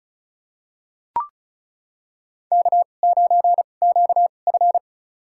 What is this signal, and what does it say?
Morse code, heard as a keyed tone, about 36 wpm, spaced as for 25 wpm, 700 Hz. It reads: K9QF
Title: USA Call Signs - 4 Characters - 2x Character Spacing - 35wpm